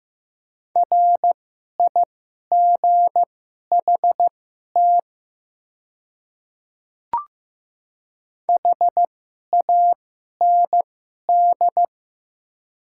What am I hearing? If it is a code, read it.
Morse code, heard as a keyed tone, about 15 wpm, spaced as for 11 wpm, 700 Hz. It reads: RIGHT HAND
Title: 2x Character Spacing - Top 300 Words - 15wpm